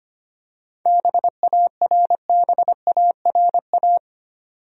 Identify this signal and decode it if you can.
Morse code, heard as a keyed tone, about 25 wpm, 700 Hz. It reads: BARBARA
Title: USA Common Names - 25wpm